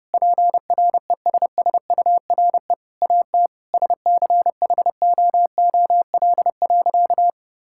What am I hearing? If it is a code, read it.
Morse code, heard as a keyed tone, about 30 wpm, 700 Hz. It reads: PRESSURE AT SCHOOL.
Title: Sentences from Top 1000 Words - Review Part 6 - 30wpm